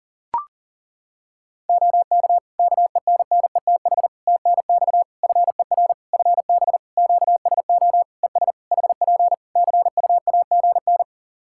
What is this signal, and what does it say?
Morse code, heard as a keyed tone, about 40 wpm, 700 Hz. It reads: OK KENNETH TNX FER FB QSO ES HP CUAGN
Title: Practice QSO - Exchange 4 - Ending - 40wpm